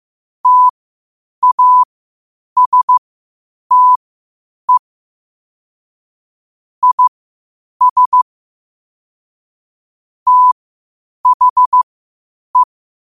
Morse code, heard as a keyed tone, about 15 wpm, spaced as for 8 wpm, 1000 Hz. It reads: TASTE IS THE
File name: qotdfiles / 8wpm